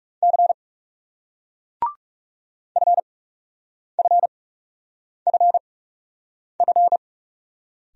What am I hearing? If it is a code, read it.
Morse code, heard as a keyed tone, about 40 wpm, spaced as for 8 wpm, 700 Hz. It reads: C FFFF